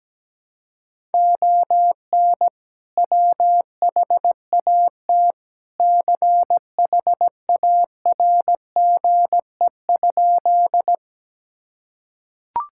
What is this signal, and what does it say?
Morse code, heard as a keyed tone, about 17 wpm, 700 Hz. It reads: ON WHAT CHARGE?
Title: Sentences from Top 1000 Words - Review Part 6 - 17wpm